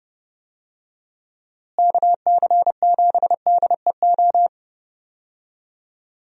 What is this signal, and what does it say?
Morse code, heard as a keyed tone, about 30 wpm, 700 Hz. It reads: KC7DEO
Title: USA Call Signs - 6 Characters - Repeated 2x - 30wpm